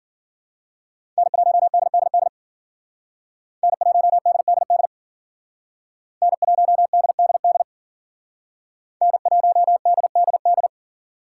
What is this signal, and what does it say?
Morse code, heard as a keyed tone, about 51 wpm, 700 Hz. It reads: N1DDD N1DDD N1DDD N1DDD